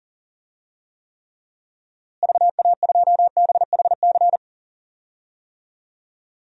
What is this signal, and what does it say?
Morse code, heard as a keyed tone, about 40 wpm, 700 Hz. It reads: VA2BHC
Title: DXCC-100 Call Signs - 6 Characters - Repeated 2x - 40wpm